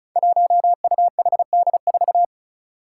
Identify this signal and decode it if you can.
Morse code, heard as a keyed tone, about 36 wpm, 700 Hz. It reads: 1UHD4
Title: Five Challenging Letter-Number Combinations 35wpm